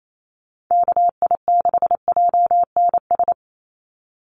Morse code, heard as a keyed tone, about 28 wpm, 700 Hz. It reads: KI6JNS
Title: USA Call Signs 28wpm